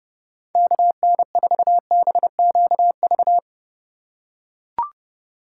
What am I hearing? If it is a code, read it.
Morse code, heard as a keyed tone, about 31 wpm, 700 Hz. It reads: KN4BQV
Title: USA Call Signs - 6 Characters - 3x Character Spacing - 30wpm